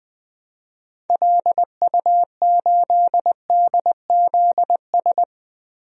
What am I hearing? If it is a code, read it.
Morse code, heard as a keyed tone, about 21 wpm, 700 Hz. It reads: LU8DZS
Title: DXCC-100 Call Signs - 6 Characters - 20wpm